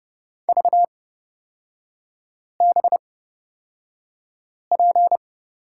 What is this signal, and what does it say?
Morse code, heard as a keyed tone, about 30 wpm, spaced as for 5 wpm, 700 Hz. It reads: VBP
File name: Lesson 36e - BT - Three Characters 30wpm